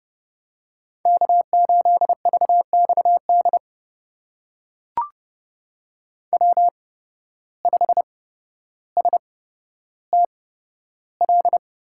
Morse code, heard as a keyed tone, about 29 wpm, spaced as for 13 wpm, 700 Hz. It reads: K8VXD W5STL
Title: USA Call Signs - 5 Characters - 8x Character Spacing - 30wpm